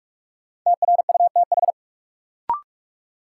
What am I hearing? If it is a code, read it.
Morse code, heard as a keyed tone, about 42 wpm, 700 Hz. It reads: TRUTH
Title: Sentences from Top 800 Words - New Word Drill - 45wpm